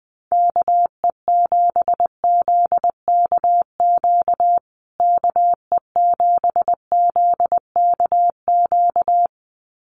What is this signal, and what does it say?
Morse code, heard as a keyed tone, about 20 wpm, 700 Hz. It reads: KE7ZKQ KE7ZKQ
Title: USA Call Signs Repeated 2x 20wpm